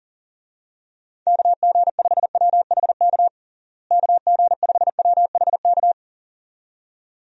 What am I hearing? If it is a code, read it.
Morse code, heard as a keyed tone, about 40 wpm, 700 Hz. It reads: KG5WHK KG5WHK
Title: USA Call Signs - 6 Characters - Repeated 2x - 40wpm